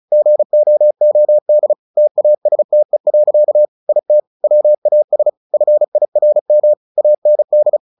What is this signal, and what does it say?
Morse code, heard as a keyed tone, about 34 wpm, 600 Hz. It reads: GOOD TASTE. IT WAS FIRM AND
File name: the old man and the sea 2 35wpm